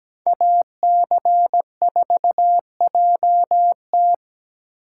Morse code, heard as a keyed tone, about 21 wpm, 700 Hz. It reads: AC4JT
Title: Speed Racing - USA 2x2 Call Signs 17wpm